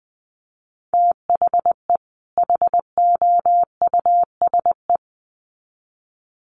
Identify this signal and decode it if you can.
Morse code, heard as a keyed tone, about 20 wpm, 700 Hz. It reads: THE HOUSE
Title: Sets of 2 Words 20wpm